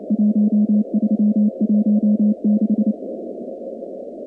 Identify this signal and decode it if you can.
Morse code, heard as a keyed tone, about 30 wpm, 210 Hz. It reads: 1316
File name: contests / 2019-cw-wpx-kt5j-k5pi- -2330 0-46